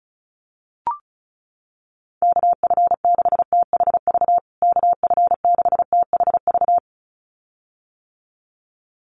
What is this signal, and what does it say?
Morse code, heard as a keyed tone, about 34 wpm, 700 Hz. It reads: KF6THV KF6THV
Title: USA Call Signs Repeated 2x 35wpm